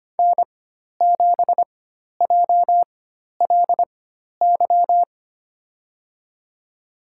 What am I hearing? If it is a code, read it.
Morse code, heard as a keyed tone, about 25 wpm, spaced as for 12 wpm, 700 Hz. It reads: N7JLY
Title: USA Call Signs - 5 Characters - 4x Character Spacing - 25wpm